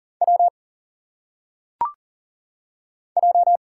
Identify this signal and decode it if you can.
Morse code, heard as a keyed tone, about 39 wpm, 700 Hz. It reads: W J
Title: Single Letters - Phonetic Pronunciation 40wpm